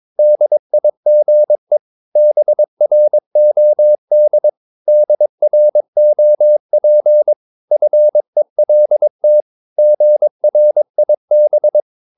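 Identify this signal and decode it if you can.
Morse code, heard as a keyed tone, about 22 wpm, 600 Hz. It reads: DIGE BROD DROP FELT GRIB
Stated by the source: Morse practice of short Danish words at 22 wpm, from CW med Gnister.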